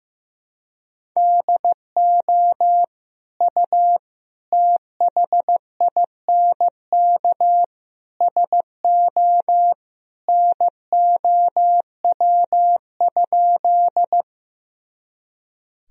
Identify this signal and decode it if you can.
Morse code, heard as a keyed tone, about 16 wpm, 700 Hz. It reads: DO U THINK SO NOW?
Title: Sentences from Top 200 Words and Abbreviations 15wpm